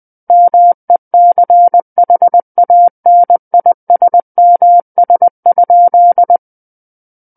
Morse code, heard as a keyed tone, about 20 wpm, 700 Hz. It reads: MECHANISMS?
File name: 20 WPM